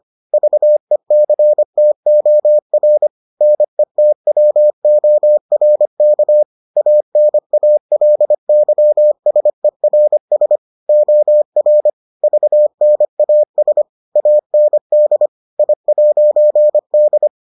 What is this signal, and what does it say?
Morse code, heard as a keyed tone, about 25 wpm, 600 Hz. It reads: VECTOR NETWORK ANALYSERS OR VNAS AND I'D